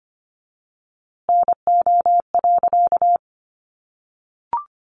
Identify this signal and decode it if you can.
Morse code, heard as a keyed tone, about 25 wpm, 700 Hz. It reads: NO.
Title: A Sound Of Thunder 25wpm